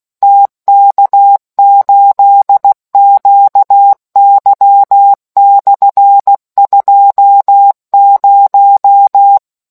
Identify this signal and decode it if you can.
Morse code, heard as a keyed tone, about 16 wpm, 800 Hz. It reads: TK8QY/20